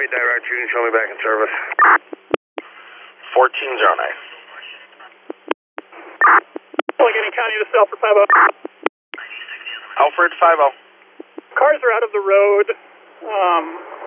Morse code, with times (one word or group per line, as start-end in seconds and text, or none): none